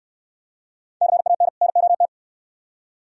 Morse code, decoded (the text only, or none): CAN NOT